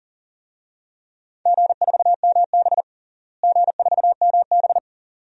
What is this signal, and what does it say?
Morse code, heard as a keyed tone, about 40 wpm, 700 Hz. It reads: G4MB G4MB
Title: DXCC-100 Call Signs - 4 Characters - Repeated 2x - 40wpm